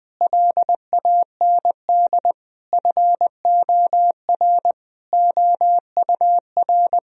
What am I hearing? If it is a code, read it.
Morse code, heard as a keyed tone, about 20 wpm, 700 Hz. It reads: LAND FOR OUR
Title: Sentences from Top 1000 Words - Review Part 4 - 20wpm